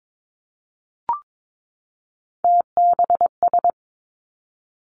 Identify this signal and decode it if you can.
Morse code, heard as a keyed tone, about 22 wpm, 700 Hz. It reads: TBS
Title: Three Challenging Letter-Number Combinations 22wpm